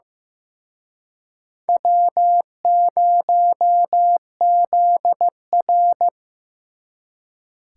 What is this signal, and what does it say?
Morse code, heard as a keyed tone, about 18 wpm, 700 Hz. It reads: W0ZR
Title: Speed Racing - USA 1x2 Call Signs 15wpm